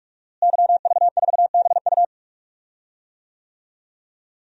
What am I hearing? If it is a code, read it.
Morse code, heard as a keyed tone, about 42 wpm, 700 Hz. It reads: YV4BU
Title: DXCC-100 Call Signs - 5 Characters - Repeated 3x - 45wpm